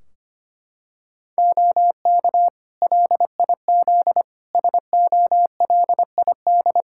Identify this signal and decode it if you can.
Morse code, heard as a keyed tone, about 25 wpm, 700 Hz. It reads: OK LIZ SOLID
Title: K1USN SST QSO Practice 25wpm